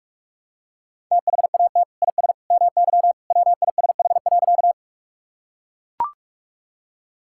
Morse code, heard as a keyed tone, about 41 wpm, 700 Hz. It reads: THAT IS MY WISH.
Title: Sentences from Top 1600 Words 45wpm